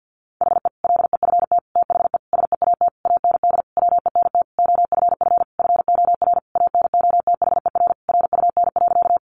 Morse code, heard as a keyed tone, about 50 wpm, 700 Hz. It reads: HE LEFT THE SEAT AND WENT OFF FOR ANOTHER RUN.
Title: Sentences from Top 1500 Words 50wpm